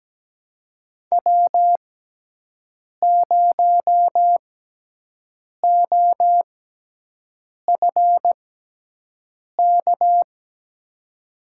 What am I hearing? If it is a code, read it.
Morse code, heard as a keyed tone, about 17 wpm, spaced as for 6 wpm, 700 Hz. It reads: W0OFK
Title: USA Call Signs - 6x Character Spacing - 17wpm